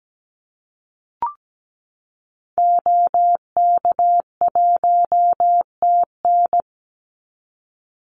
Morse code, heard as a keyed tone, about 17 wpm, 700 Hz. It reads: OK1TN
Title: Call Signs Repeated 2x - 17wpm